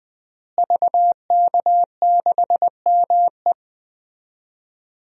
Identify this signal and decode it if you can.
Morse code, heard as a keyed tone, about 20 wpm, 700 Hz. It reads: VK6ME
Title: DXCC-100 Call Signs - 5 Characters - Repeated 2x - 20wpm